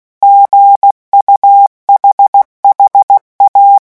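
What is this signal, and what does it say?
Morse code, heard as a keyed tone, about 16 wpm, 800 Hz. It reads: GUHHA
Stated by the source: Morse letter-group revision drill at 15 words per minute.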